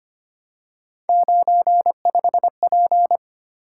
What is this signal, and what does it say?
Morse code, heard as a keyed tone, about 26 wpm, 700 Hz. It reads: 95P
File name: Three Letter-Number Combinations - 25wpm